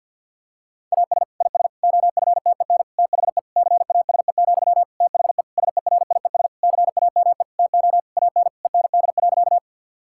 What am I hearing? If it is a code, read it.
Morse code, heard as a keyed tone, about 50 wpm, 700 Hz. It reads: AS IS OFTEN THE CASE, THE SERIES CAME TO AN END.
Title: Sentences from Top 1000 Words - Review Part 5 - 50wpm